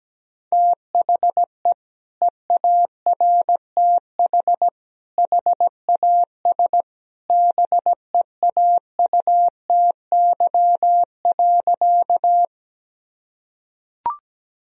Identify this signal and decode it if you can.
Morse code, heard as a keyed tone, about 17 wpm, 700 Hz. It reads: THE EARTH HAS BEAUTY.
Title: Sentences from Top 1000 Words - Review Part 2 - 17wpm